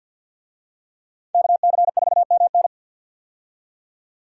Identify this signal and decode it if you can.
Morse code, heard as a keyed tone, about 53 wpm, 700 Hz. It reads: KC4MN
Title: USA Call Signs - 5 Characters - 50wpm